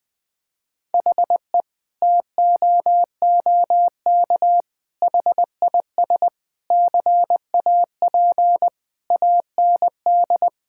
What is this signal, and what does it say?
Morse code, heard as a keyed tone, about 21 wpm, 700 Hz. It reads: HE TOOK HIS CAP AND
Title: Sentences from Top 1600 Words 20wpm